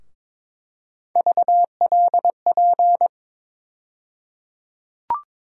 Morse code, heard as a keyed tone, about 22 wpm, 700 Hz. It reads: VLP
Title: USA Call Sign Postfixes - 3 Letters 22wpm